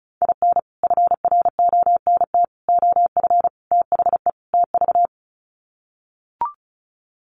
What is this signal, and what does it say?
Morse code, heard as a keyed tone, about 34 wpm, 700 Hz. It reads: IN FRONT OF THE TV
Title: Sets of 5 Words Encore2 35wpm